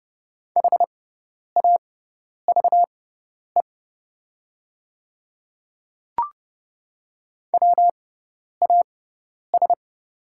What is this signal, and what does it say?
Morse code, heard as a keyed tone, about 30 wpm, spaced as for 10 wpm, 700 Hz. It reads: HAVE WAS
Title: 6x Character Spacing - Top 200 Words - 30wpm